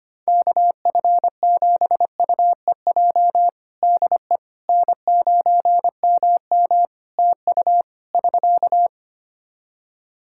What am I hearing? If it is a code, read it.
Morse code, heard as a keyed tone, about 25 wpm, 700 Hz. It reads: KF7UEJ DE N9MM TU <SK>
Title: Practice QSO - Exchange 4 - Ending - 25wpm